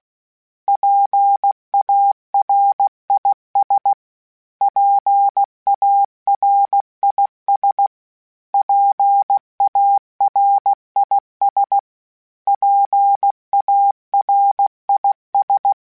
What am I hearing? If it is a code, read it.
Morse code, heard as a keyed tone, about 16 wpm, 800 Hz. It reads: PARIS PARIS PARIS PARIS
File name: PARIS Calibration / PARIS mp3s / PARIS@15